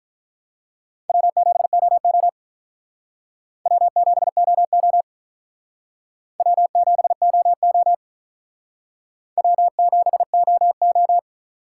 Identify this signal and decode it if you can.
Morse code, heard as a keyed tone, about 43 wpm, 700 Hz. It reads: W7OO W7OO W7OO W7OO
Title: Speed Racing - USA 1x2 Call Signs 35wpm